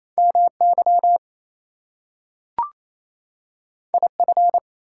Morse code, heard as a keyed tone, about 28 wpm, 700 Hz. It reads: MY IF
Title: Two Letter Words - 28wpm